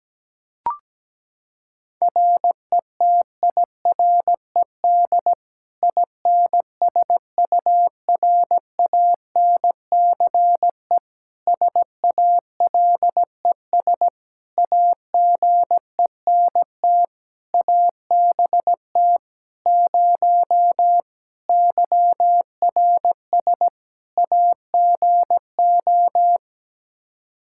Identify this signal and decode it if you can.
Morse code, heard as a keyed tone, about 17 wpm, 700 Hz. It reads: RETIRED INSURANCE SALES AGENT ABT 0 YRS AGO
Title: Practice QSO - Exchange 3 - More Info - 17wpm